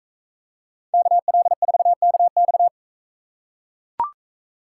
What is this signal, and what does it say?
Morse code, heard as a keyed tone, about 42 wpm, 700 Hz. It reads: KL4KX